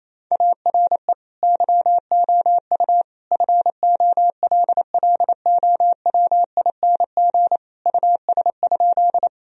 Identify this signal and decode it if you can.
Morse code, heard as a keyed tone, about 28 wpm, 700 Hz. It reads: ARE YOU FOLLOWING US?
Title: Sentences from Top 1000 Words - Review Part 4 - 28wpm